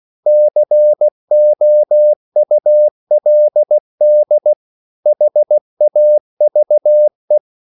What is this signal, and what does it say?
Morse code, heard as a keyed tone, about 16 wpm, 600 Hz. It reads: COULD HAVE